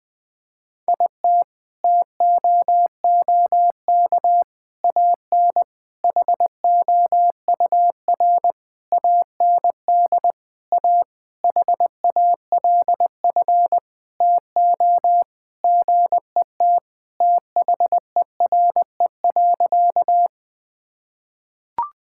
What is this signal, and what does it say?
Morse code, heard as a keyed tone, about 21 wpm, 700 Hz. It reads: IT TOOK AN HOUR AND A HALF TO GET THERE.
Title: Sentences with 5-Word Phrases - 20wpm